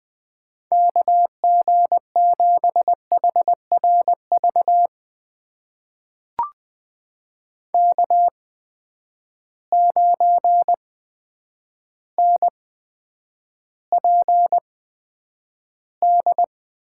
Morse code, heard as a keyed tone, about 20 wpm, spaced as for 9 wpm, 700 Hz. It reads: KG7HRV K9NPD
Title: USA Call Signs - 8x Character Spacing - 20wpm